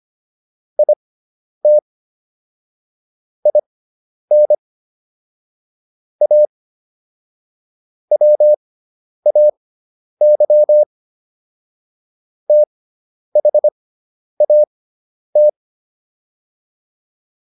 Morse code, heard as a keyed tone, about 25 wpm, spaced as for 10 wpm, 600 Hz. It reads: IT IN A WAY THAT